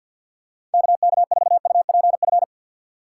Morse code, heard as a keyed tone, about 53 wpm, 700 Hz. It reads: K K 4 U P F
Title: USA Call Signs - 6 Characters - 6x Character Spacing - 50wpm